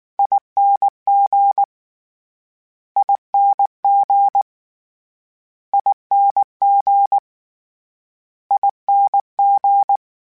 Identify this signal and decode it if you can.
Morse code, heard as a keyed tone, about 19 wpm, 800 Hz. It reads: ING ING ING ING